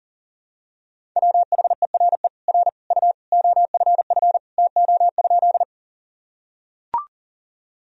Morse code, heard as a keyed tone, about 40 wpm, 700 Hz. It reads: WHERE R U OFF TO?
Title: Sentences from Top 200 Words and Abbreviations 40wpm